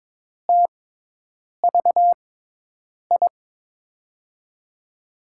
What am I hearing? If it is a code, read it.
Morse code, heard as a keyed tone, about 22 wpm, spaced as for 8 wpm, 700 Hz. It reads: TVI
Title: USA Call Signs - 6x Character Spacing - 22wpm